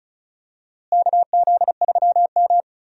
Morse code, heard as a keyed tone, about 36 wpm, 700 Hz. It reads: KZ3M
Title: USA Call Signs - 4 Characters - Repeated 2x - 35wpm